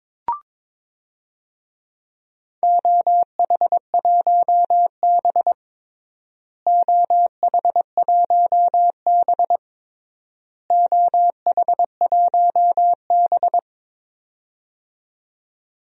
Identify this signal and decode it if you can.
Morse code, heard as a keyed tone, about 22 wpm, 700 Hz. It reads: OH1B OH1B OH1B
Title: DXCC-100 Call Signs - 4 Characters - Repeated 3x - 22wpm